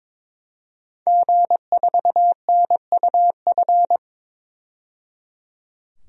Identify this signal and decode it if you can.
Morse code, heard as a keyed tone, about 22 wpm, 700 Hz. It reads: G4NUF
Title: DXCC-100 Call Signs - 5 Characters - 22wpm